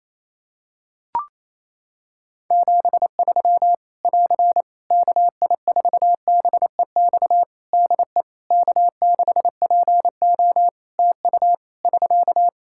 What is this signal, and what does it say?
Morse code, heard as a keyed tone, about 28 wpm, 700 Hz. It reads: 73 <AR> KI4BEX DE K6PO TU <SK>
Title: Practice QSO - Exchange 4 - Ending - 28wpm